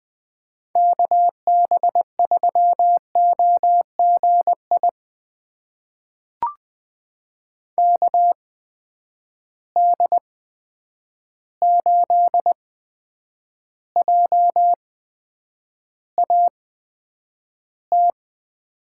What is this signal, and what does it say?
Morse code, heard as a keyed tone, about 20 wpm, spaced as for 9 wpm, 700 Hz. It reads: KB3OGI KD8JAT
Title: USA Call Signs - 6 Characters - 8x Character Spacing - 20wpm